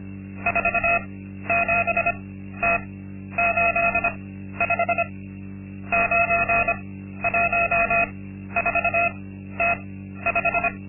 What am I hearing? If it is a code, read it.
Morse code, heard as a keyed tone, about 25 wpm, spaced as for 12 wpm, 690 Hz. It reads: 47T85 914T5
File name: Linksmakalnis-2